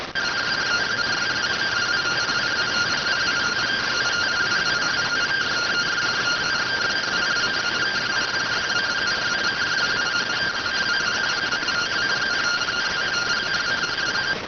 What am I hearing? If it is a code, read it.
Two-tone teleprinter signal, about 170 Hz shift.